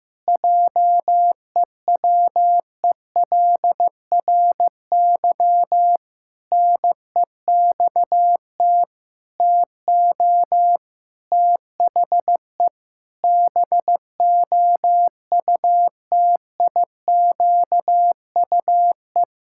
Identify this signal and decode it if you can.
Morse code, heard as a keyed tone, about 15 wpm, 700 Hz. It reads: JEWELRY NEXT TO THE BOUTIQUE
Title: Pangram Challenge v2 - 15wpm